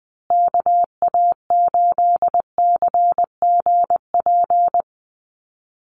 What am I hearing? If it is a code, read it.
Morse code, heard as a keyed tone, about 20 wpm, 700 Hz. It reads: KA8CGP
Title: USA Call Signs Repeated 2x 20wpm